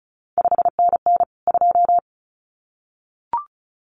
Morse code, heard as a keyed tone, about 36 wpm, 700 Hz. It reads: 5NN 2
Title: CQ WW DX Contest Exchange Encore - 35wpm